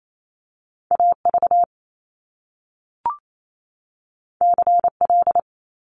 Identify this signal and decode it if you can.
Morse code, heard as a keyed tone, about 28 wpm, 700 Hz. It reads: AV CL
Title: Top Two Letter Combinations in Words 28wpm